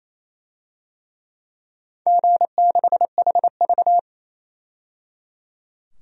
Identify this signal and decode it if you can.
Morse code, heard as a keyed tone, about 28 wpm, 700 Hz. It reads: G6HV
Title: DXCC-100 Call Signs - 4 Characters - 28wpm